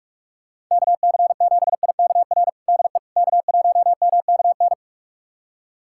Morse code, heard as a keyed tone, about 42 wpm, 700 Hz. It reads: KC7IKR DE K1MKN
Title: USA Callsign DE Callsign 45wpm